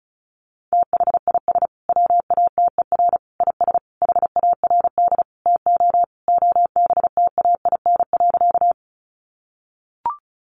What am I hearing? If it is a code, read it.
Morse code, heard as a keyed tone, about 34 wpm, 700 Hz. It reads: THIS WATER IS HARD TO OBTAIN.
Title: Sentences from Top 1300 Words 35wpm